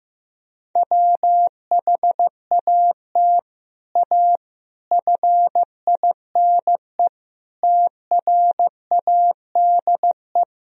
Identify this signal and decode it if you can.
Morse code, heard as a keyed tone, about 15 wpm, 700 Hz. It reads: WHAT A FINE TRADE
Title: Sentences from Top 1000 Words - Review Part 4 - 15wpm